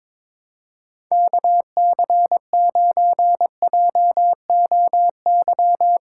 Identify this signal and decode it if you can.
Morse code, heard as a keyed tone, about 23 wpm, 700 Hz. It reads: KC9JOY
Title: USA Callsign DE Callsign - 6 Characters - 22wpm